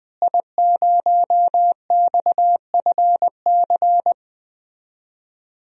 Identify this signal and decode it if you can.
Morse code, heard as a keyed tone, about 20 wpm, 700 Hz. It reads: I0XFC
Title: DXCC-100 Call Signs - 5 Characters - Repeated 3x - 20wpm